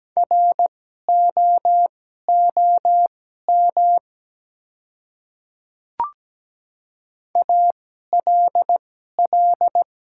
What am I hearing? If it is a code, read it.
Morse code, heard as a keyed tone, about 17 wpm, spaced as for 12 wpm, 700 Hz. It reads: ROOM ALL
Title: 2x Character Spacing - Top 300 Words - 17wpm